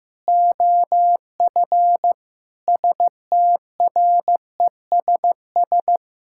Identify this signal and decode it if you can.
Morse code, heard as a keyed tone, about 15 wpm, 700 Hz. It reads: OF STRESS
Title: Sentences from Top 1000 Words - Review Part 6 - 15wpm